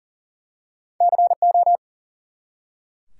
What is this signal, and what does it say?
Morse code, heard as a keyed tone, about 39 wpm, 700 Hz. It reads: CO
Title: US States and Canadian Provinces 40wpm